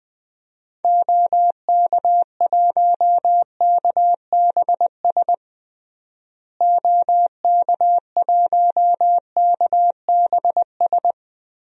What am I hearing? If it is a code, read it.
Morse code, heard as a keyed tone, about 20 wpm, 700 Hz. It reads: OK1KBS OK1KBS
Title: DXCC-100 Call Signs - 6 Characters - Repeated 2x - 20wpm